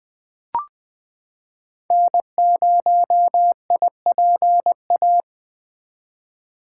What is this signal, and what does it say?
Morse code, heard as a keyed tone, about 20 wpm, 700 Hz. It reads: N0IPA
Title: 2023 SOTA Chaser Call Signs - 20wpm